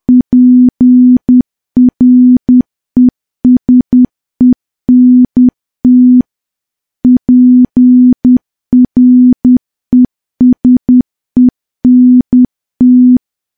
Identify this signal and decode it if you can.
Morse code, heard as a keyed tone, about 10 wpm, 250 Hz. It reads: PRESENT PRESENT
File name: morse / morse-training-10WPM